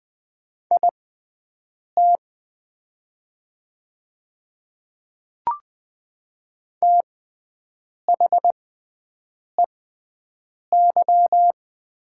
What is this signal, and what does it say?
Morse code, heard as a keyed tone, about 20 wpm, spaced as for 7 wpm, 700 Hz. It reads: IT THEY
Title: 6x Character Spacing - Top 200 Words - 20wpm